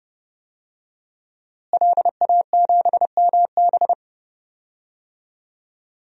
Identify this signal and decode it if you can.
Morse code, heard as a keyed tone, about 32 wpm, 700 Hz. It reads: LA7MB
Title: DXCC-100 Call Signs - 5 Characters - Repeated 2x - 30wpm